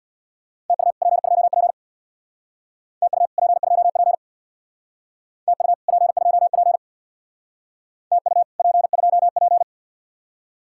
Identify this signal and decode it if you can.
Morse code, heard as a keyed tone, about 62 wpm, 700 Hz. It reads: TU P2P TU P2P TU P2P TU P2P
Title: Speed-Racing Common QSO Phrases - 50wpm